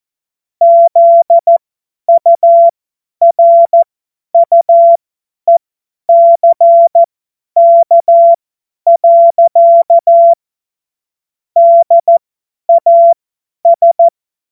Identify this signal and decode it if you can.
Morse code, heard as a keyed tone, about 14 wpm, spaced as for 10 wpm, 680 Hz. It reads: ZURUECK. DAS
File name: Morsekurs / Witze / Witz 21.November 2023 50 F70